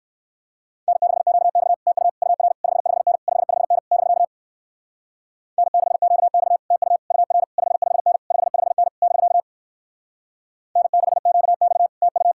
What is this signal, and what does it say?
Morse code, heard as a keyed tone, about 59 wpm, 700 Hz. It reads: N6QX TU UR 55N 55N <BK> N6QX TU UR 55N 55N <BK> N6QX TU